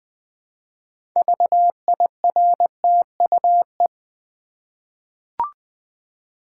Morse code, heard as a keyed tone, about 20 wpm, 700 Hz. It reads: VIRTUE
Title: 6 Letter Words - First 3 Letters Given - 20wpm